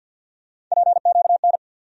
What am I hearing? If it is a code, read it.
Morse code, heard as a keyed tone, about 53 wpm, 700 Hz. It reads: PQN
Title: USA Call Sign Postfixes - 3 Letters 50wpm